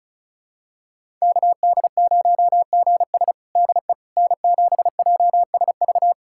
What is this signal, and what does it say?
Morse code, heard as a keyed tone, about 36 wpm, 700 Hz. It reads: KD0GS DE N7JSV
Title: USA Callsign DE Callsign - 5 Characters - 35wpm